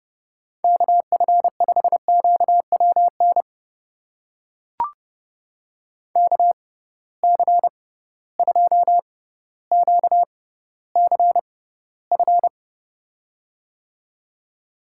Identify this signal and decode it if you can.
Morse code, heard as a keyed tone, about 30 wpm, spaced as for 15 wpm, 700 Hz. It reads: KF5QWN KC2QCF
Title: USA Call Signs - 6 Characters - 6x Character Spacing - 30wpm